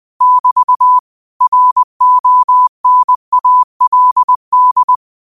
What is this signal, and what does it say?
Morse code, heard as a keyed tone, about 20 wpm, 1000 Hz. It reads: <BT> RONALD
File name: qotdfiles / 20wpm